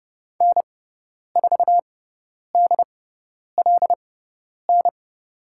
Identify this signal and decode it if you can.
Morse code, heard as a keyed tone, about 30 wpm, spaced as for 10 wpm, 700 Hz. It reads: N4DLN
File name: Lesson 16g - Letter c - Call Signs 30wpm